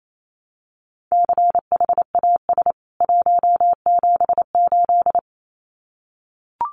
Morse code, heard as a keyed tone, about 28 wpm, 700 Hz. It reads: CHAS 178
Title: CWT Exchange 28wpm